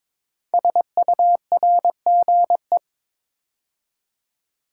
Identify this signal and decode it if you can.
Morse code, heard as a keyed tone, about 22 wpm, 700 Hz. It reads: SURGE
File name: 5 Letter Words - First 3 Letters Given - 22wpm